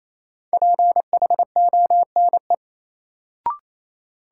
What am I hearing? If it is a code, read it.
Morse code, heard as a keyed tone, about 28 wpm, 700 Hz. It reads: PHONE
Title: Sentences from Top 1100 Words - New Word Drill - 28wpm